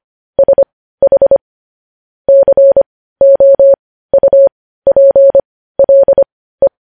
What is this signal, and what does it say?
Morse code, heard as a keyed tone, about 25 wpm, spaced as for 15 wpm, 550 Hz. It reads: SH COUPLE
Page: news headlines in morse code at 15 wpm